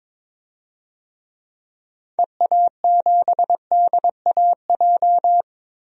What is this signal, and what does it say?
Morse code, heard as a keyed tone, about 22 wpm, 700 Hz. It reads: EA7DAJ
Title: DXCC-100 Call Signs - 6 Characters - Repeated 2x - 22wpm